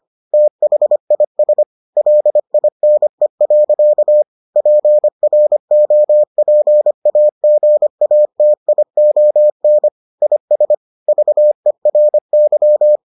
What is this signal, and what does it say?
Morse code, heard as a keyed tone, about 25 wpm, 600 Hz. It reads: THIS LINE. PROPAGATION IS VERY